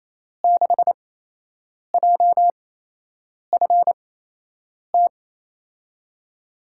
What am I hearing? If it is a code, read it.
Morse code, heard as a keyed tone, about 28 wpm, spaced as for 8 wpm, 700 Hz. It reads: 6JFT
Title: USA Call Signs - 5 Characters - 8x Character Spacing - 28wpm